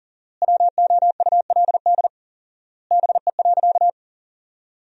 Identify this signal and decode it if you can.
Morse code, heard as a keyed tone, about 37 wpm, 700 Hz. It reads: WOULD BE.
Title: 4x Word-Spacing - Sentences from Top 200 Words - 40wpm